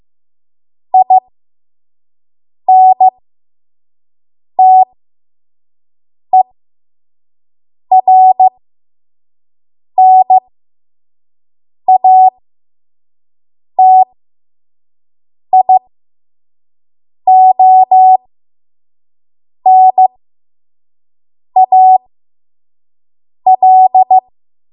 Morse code, heard as a keyed tone, about 15 wpm, spaced as for 5 wpm, 750 Hz. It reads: INTERNATIONAL
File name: Morse / 05WPM